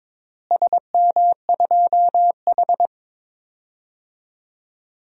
Morse code, heard as a keyed tone, about 22 wpm, 700 Hz. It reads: SM2H
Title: DXCC-100 Call Signs - 4 Characters - Repeated 3x - 22wpm